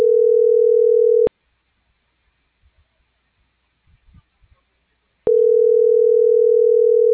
Ringback tone (North America), 440 + 480 Hz.